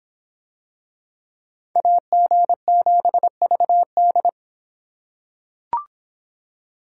Morse code, heard as a keyed tone, about 26 wpm, 700 Hz. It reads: AG7VD